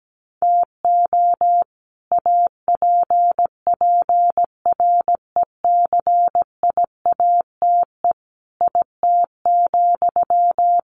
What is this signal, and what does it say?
Morse code, heard as a keyed tone, about 17 wpm, 700 Hz. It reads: TO APPRECIATE IT,
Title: Sentences from Top 1300 Words 17wpm